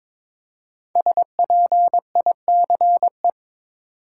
Morse code, heard as a keyed tone, about 22 wpm, 700 Hz. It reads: SPICE
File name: 5 Letter Words - First 3 Letters Given - 22wpm